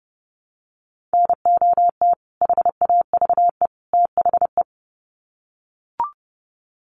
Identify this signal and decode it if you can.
Morse code, heard as a keyed tone, about 28 wpm, 700 Hz. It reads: NOT HAVE THE
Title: Sets of 3 Words 30wpm